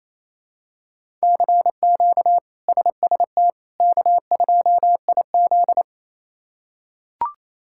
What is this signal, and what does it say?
Morse code, heard as a keyed tone, about 28 wpm, 700 Hz. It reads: CQ SST K2IZ
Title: K1USN SST QSO Practice - 28wpm